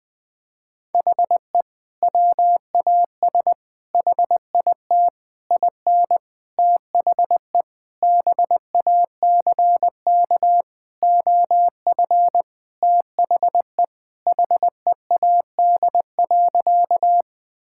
Morse code, heard as a keyed tone, about 21 wpm, 700 Hz. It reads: HE WAS HIT IN THE BACK OF THE HEAD.
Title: Sentences with 6-Word Phrases - Spoken Beginning - 20wpm